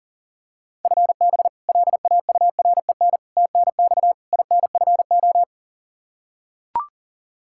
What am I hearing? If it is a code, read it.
Morse code, heard as a keyed tone, about 40 wpm, 700 Hz. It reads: FB LAUREN TNX INFO
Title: K1USN SST QSO Practice 40wpm